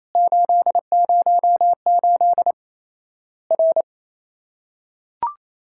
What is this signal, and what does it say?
Morse code, heard as a keyed tone, about 28 wpm, 700 Hz. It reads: 808 R
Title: Worked All Europe DX Contest - QTC Exchange - 28wpm